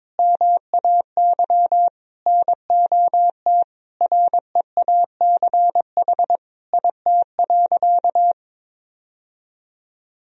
Morse code, heard as a keyed tone, about 22 wpm, 700 Hz. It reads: MAY NOT REACH IT.
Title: Sentences from Top 1000 Words - Review Part 2 - 22wpm